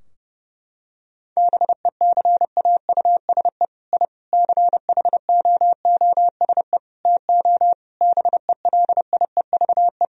Morse code, heard as a keyed tone, about 28 wpm, 700 Hz. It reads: BECAUSE I CHOOSE TO BELIEVE
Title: Sentences from Top 1000 Words - Review Part 4 - 30wpm